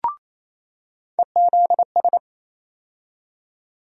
Morse code, heard as a keyed tone, about 28 wpm, 700 Hz. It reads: EZS
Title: USA Call Sign Postfixes - 3 Letters 28wpm